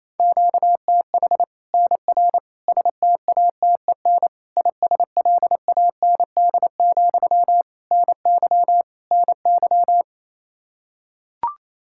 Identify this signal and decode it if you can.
Morse code, heard as a keyed tone, about 28 wpm, 700 Hz. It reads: QTH NR STATEN ISLAND, NY NY
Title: Practice QSO - Exchange 1 - Beginning - 28wpm